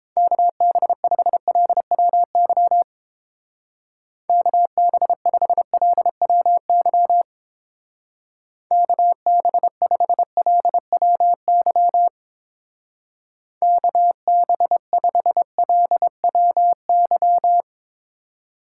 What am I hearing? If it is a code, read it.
Morse code, heard as a keyed tone, about 28 wpm, 700 Hz. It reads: KB5LWY KB5LWY KB5LWY KB5LWY